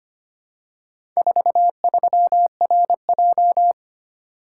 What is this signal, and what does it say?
Morse code, heard as a keyed tone, about 25 wpm, 690 Hz. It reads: 43RJ
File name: practice / 25wpm / Four Letter-Number Combinations - 25wpm